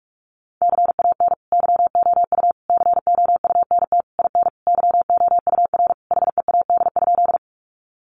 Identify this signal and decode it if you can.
Morse code, heard as a keyed tone, about 44 wpm, 700 Hz. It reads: CAN YOU COUNT IN YOUR HEAD?
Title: Sentences from Top 1500 Words 45wpm